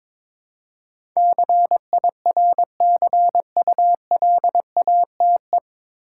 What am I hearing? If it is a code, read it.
Morse code, heard as a keyed tone, about 22 wpm, 700 Hz. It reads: CIRCULATE
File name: 9 Letter Words - First 3 Letters Given - 22wpm